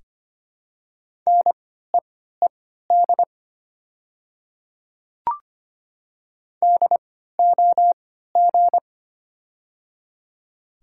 Morse code, heard as a keyed tone, about 25 wpm, spaced as for 14 wpm, 700 Hz. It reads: NEED DOG
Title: 3x Character Spacing - Top 500 Words - 25wpm